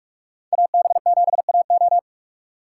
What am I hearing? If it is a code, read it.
Morse code, heard as a keyed tone, about 46 wpm, 700 Hz. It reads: AB7AO
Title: USA Call Signs - 5 Characters - 2x Character Spacing - 45wpm